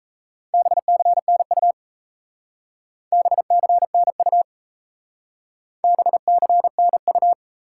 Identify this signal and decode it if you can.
Morse code, heard as a keyed tone, about 34 wpm, 700 Hz. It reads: BCNU BCNU BCNU
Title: Speed-Racing Top 90 QSO Elements 28wpm